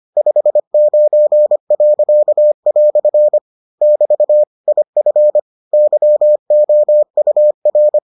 Morse code, heard as a keyed tone, about 25 wpm, 600 Hz. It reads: 59." <BT> IF YOUR